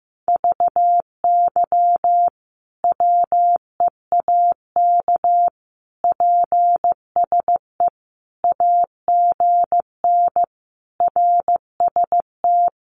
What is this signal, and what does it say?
Morse code, heard as a keyed tone, about 15 wpm, 700 Hz. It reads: VY WEAK PSE AGN RST
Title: Common QSO Phrases 15wpm